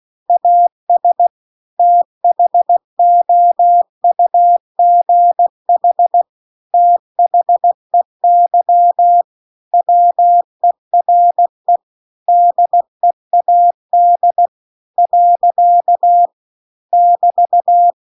Morse code, heard as a keyed tone, about 16 wpm, 700 Hz. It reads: AS THOUGH THEY WERE DEAD . <BT>
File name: CW-The-Big-Twoheaded-River-2-16WPM